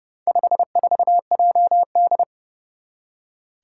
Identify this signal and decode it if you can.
Morse code, heard as a keyed tone, about 30 wpm, 700 Hz. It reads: 54JD
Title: Four Challenging Letter-Number Combinations 30wpm